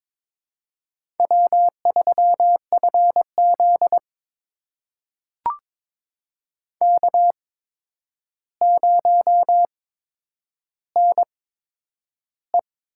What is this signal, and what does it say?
Morse code, heard as a keyed tone, about 22 wpm, spaced as for 9 wpm, 700 Hz. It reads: W3FZ K0NE
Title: USA Call Signs - 4 Characters - 8x Character Spacing - 22wpm